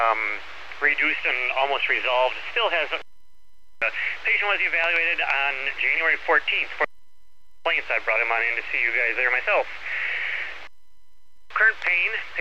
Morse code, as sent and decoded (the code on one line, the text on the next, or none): none